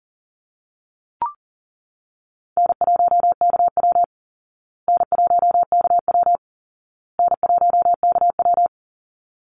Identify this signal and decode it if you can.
Morse code, heard as a keyed tone, about 37 wpm, 700 Hz. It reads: N1KW N1KW N1KW
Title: Call Signs Repeated 2x - 40wpm